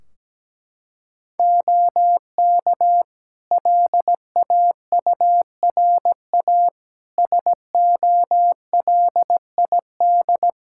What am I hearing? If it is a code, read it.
Morse code, heard as a keyed tone, about 17 wpm, 700 Hz. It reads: OK LAURA SOLID
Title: K1USN SST QSO Practice 17wpm